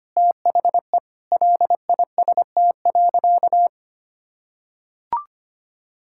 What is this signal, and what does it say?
Morse code, heard as a keyed tone, about 25 wpm, 700 Hz. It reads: THE LIST.
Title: Sentences from Top 1000 Words - Review Part 6 - 25wpm